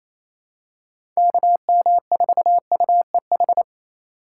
Morse code, heard as a keyed tone, about 28 wpm, 700 Hz. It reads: KM4UEH
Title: USA Call Signs - 6 Characters - 2x Character Spacing - 28wpm